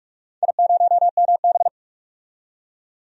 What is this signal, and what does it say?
Morse code, heard as a keyed tone, about 44 wpm, 700 Hz. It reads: I0MB
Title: DXCC-100 Call Signs - 4 Characters - 45wpm